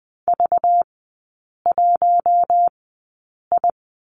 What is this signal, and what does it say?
Morse code, heard as a keyed tone, about 20 wpm, spaced as for 8 wpm, 700 Hz. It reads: V1I
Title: 205 Lesson 33e - Letter j - Three Characters 20wpm